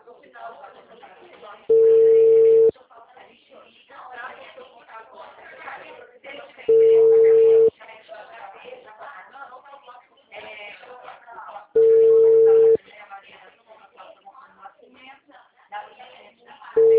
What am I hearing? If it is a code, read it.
Ringback tone (China, 450 Hz), with speech.